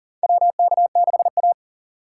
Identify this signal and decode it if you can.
Morse code, heard as a keyed tone, about 40 wpm, 700 Hz. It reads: WK6A
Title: USA Call Signs - 4 Characters - 2x Character Spacing - 40wpm